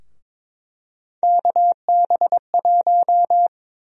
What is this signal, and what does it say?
Morse code, heard as a keyed tone, about 21 wpm, 700 Hz. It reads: KB1
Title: USA Call Sign Prefixes - 22wpm